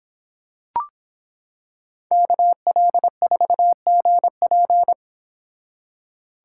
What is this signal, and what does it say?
Morse code, heard as a keyed tone, about 26 wpm, 700 Hz. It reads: KL4GP